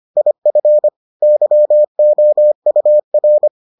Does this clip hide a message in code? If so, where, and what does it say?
Yes: Morse code from 0.2 to 3.5 s, heard as a keyed tone, about 25 wpm, 600 Hz. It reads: IF YOUR